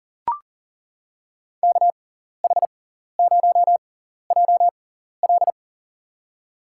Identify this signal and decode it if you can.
Morse code, heard as a keyed tone, about 40 wpm, spaced as for 14 wpm, 700 Hz. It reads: KH0JL
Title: USA Call Signs - 5 Characters - 6x Character Spacing - 40wpm